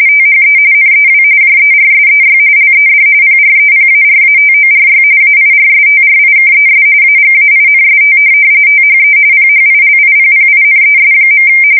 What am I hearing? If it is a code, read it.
RTTY, about 45.45 baud. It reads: THE NEW JERSEY LED PREPARATORY AIRSTRIKES AGAINST THE PHILIPPINES.
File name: fsk170 / USS-New-Jersey